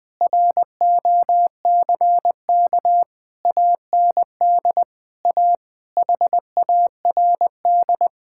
Morse code, heard as a keyed tone, about 20 wpm, 700 Hz. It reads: ROCK AND A HARD
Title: Sets of 8 Words - 20wpm